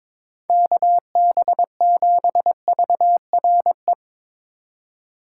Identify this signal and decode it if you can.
Morse code, heard as a keyed tone, about 22 wpm, 700 Hz. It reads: KB7VRE
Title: USA Call Signs - 6 Characters - 3x Character Spacing - 22wpm